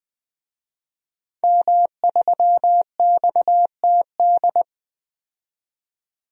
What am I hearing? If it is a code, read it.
Morse code, heard as a keyed tone, about 20 wpm, 700 Hz. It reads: M3XTD
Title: DXCC-100 Call Signs - 5 Characters - 20wpm